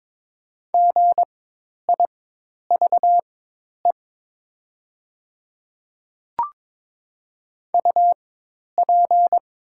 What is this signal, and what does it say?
Morse code, heard as a keyed tone, about 22 wpm, spaced as for 10 wpm, 700 Hz. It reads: GIVE UP
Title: 4x Character Spacing - Top 200 Words - 22wpm